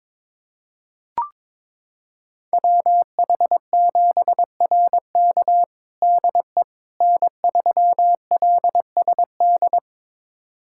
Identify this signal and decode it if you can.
Morse code, heard as a keyed tone, about 21 wpm, 700 Hz. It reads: WH7RK DE N3LSD
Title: USA Callsign DE Callsign - 5 Characters - 22wpm